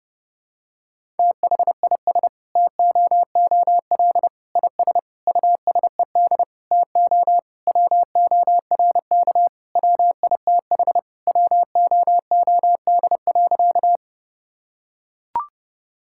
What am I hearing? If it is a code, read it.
Morse code, heard as a keyed tone, about 30 wpm, 700 Hz. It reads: THIS TOOL IS USED TO WORK WITH WOOD.
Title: Sentences from Top 1000 Words - Review Part 6 - 30wpm